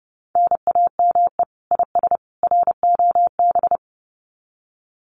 Morse code, heard as a keyed tone, about 28 wpm, 700 Hz. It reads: NAME IS ROB
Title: Common QSO Phrases 30wpm